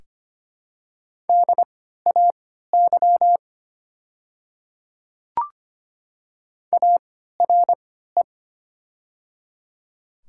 Morse code, heard as a keyed tone, about 25 wpm, spaced as for 14 wpm, 700 Hz. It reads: DAY ARE
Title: 3x Character Spacing - Top 100 Words - 25wpm